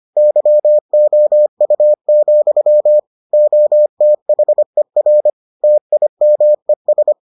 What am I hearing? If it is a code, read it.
Morse code, heard as a keyed tone, about 25 wpm, 600 Hz. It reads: YOU, OTHER TIMES